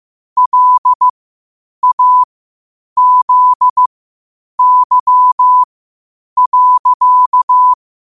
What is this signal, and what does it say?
Morse code, heard as a keyed tone, about 15 wpm, spaced as for 8 wpm, 1000 Hz. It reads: LAZY.
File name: qotdfiles / 8wpm